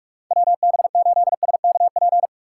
Morse code, heard as a keyed tone, about 45 wpm, 700 Hz. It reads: WB8SKP
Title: K1USN SST QSO Practice - 45wpm